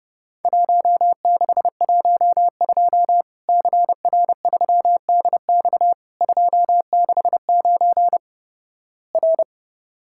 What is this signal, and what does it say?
Morse code, heard as a keyed tone, about 30 wpm, 700 Hz. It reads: 1612 CR3DX 269 R
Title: Worked All Europe DX Contest - QTC Exchange - 30wpm